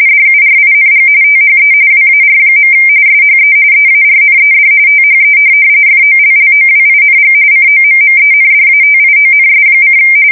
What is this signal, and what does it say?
RTTY, about 45.45 baud. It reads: DURING THE WAR. SHE EARNED THE FOLLOWING COMMENDATIONS:
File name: fsk170 / USS-New-Jersey